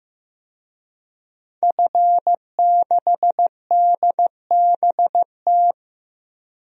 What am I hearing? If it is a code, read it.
Morse code, heard as a keyed tone, about 15 wpm, 700 Hz. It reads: F6DBT
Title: DXCC-100 Call Signs - 5 Characters - 15wpm